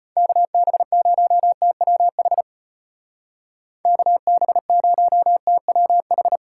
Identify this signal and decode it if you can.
Morse code, heard as a keyed tone, about 33 wpm, 700 Hz. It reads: KB0TWH KB0TWH